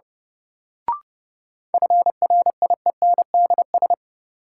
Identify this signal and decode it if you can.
Morse code, heard as a keyed tone, about 30 wpm, 700 Hz. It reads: FRIENDS
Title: Sentences from Top 1000 Words - Review Part 5 - New Word Drill - 30wpm